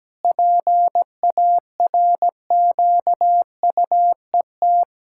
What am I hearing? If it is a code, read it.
Morse code, heard as a keyed tone, about 17 wpm, 700 Hz. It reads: PARQUET
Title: Pangram Challenge v2 - 17wpm